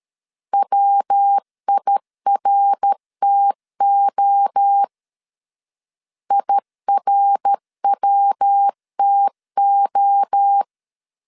Morse code, heard as a keyed tone, about 13 wpm, 800 Hz. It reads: WIRTO IRWTO
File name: CW / DET / 1 3 1500C S C 12wpm 2750